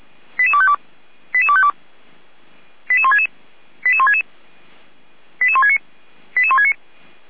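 Five-tone selective call (ZVEI), heard as quick runs of five tones, tones of about 70 ms. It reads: 88252 88252 88166 88166 88169 88169
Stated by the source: ZVEI five-tone call received with an SDR, from fire brigade Feuerwehr Orlamünde.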